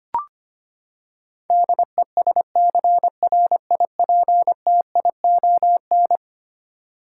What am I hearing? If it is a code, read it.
Morse code, heard as a keyed tone, about 25 wpm, 700 Hz. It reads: DESCRIPTION
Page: morse-code-ninja-practice-25wpm